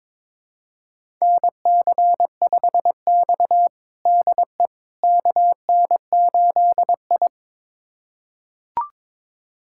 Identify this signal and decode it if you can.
Morse code, heard as a keyed tone, about 22 wpm, 700 Hz. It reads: NC5X DE KN8I
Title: USA Callsign DE Callsign - 4 Characters - 22wpm